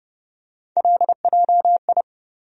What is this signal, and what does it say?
Morse code, heard as a keyed tone, about 32 wpm, 700 Hz. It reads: LJI
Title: USA Call Sign Postfixes - 3 Letters 30wpm